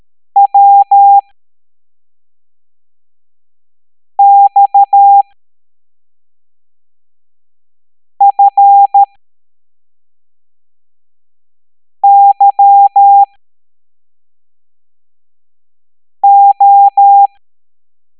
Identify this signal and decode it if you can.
Morse code, heard as a keyed tone, about 13 wpm, spaced as for 3 wpm, 800 Hz. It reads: WXFYO